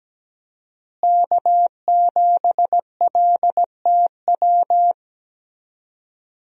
Morse code, heard as a keyed tone, about 17 wpm, 700 Hz. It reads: K7LTW
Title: USA Call Signs - 5 Characters - Repeated 2x - 17wpm